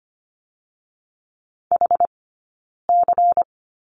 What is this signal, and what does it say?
Morse code, heard as a keyed tone, about 25 wpm, 700 Hz. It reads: H C
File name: course / 25:10wpm / 092 Lesson 16d - Letter c - Two Characters 25wpm